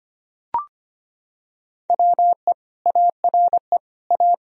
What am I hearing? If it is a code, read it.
Morse code, heard as a keyed tone, about 25 wpm, 700 Hz. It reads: WE ARE A
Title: Sentences from Top 1000 Words - Review Part 4 - 25wpm